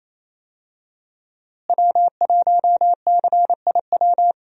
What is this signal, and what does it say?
Morse code, heard as a keyed tone, about 28 wpm, 700 Hz. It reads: W1CIW
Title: USA Call Signs - 5 Characters - Repeated 2x - 28wpm